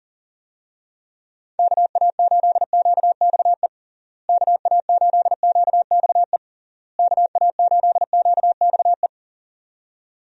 Morse code, heard as a keyed tone, about 37 wpm, 700 Hz. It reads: KA8QXE KA8QXE KA8QXE
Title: USA Call Signs - 6 Characters - Repeated 3x - 40wpm